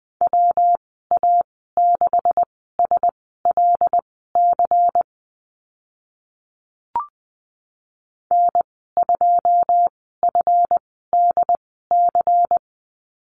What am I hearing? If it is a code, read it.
Morse code, heard as a keyed tone, about 20 wpm, spaced as for 15 wpm, 700 Hz. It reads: WA6SLC N2FDC
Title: 161 Lesson 26g - Letter g - Call Signs 20wpm